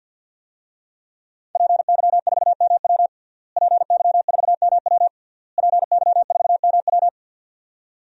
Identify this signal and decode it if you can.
Morse code, heard as a keyed tone, about 52 wpm, 700 Hz. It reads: PY4MW PY4MW PY4MW
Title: DXCC-100 Call Signs - 5 Characters - Repeated 3x - 50wpm